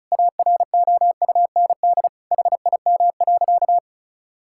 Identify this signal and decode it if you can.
Morse code, heard as a keyed tone, about 36 wpm, 700 Hz. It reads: AROUND HIM.
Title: Sentences from Top 1000 Words - Review Part 6 - 35wpm